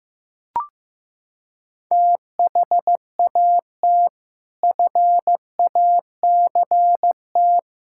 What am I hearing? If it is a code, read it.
Morse code, heard as a keyed tone, about 16 wpm, 700 Hz. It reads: THAT FACT
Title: Sentences from Top 1000 Words - Review Part 6 - 15wpm